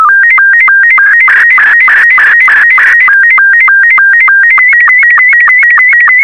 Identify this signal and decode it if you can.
SSTV (slow-scan television), Robot 72.
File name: sstv / R72-VK4MSL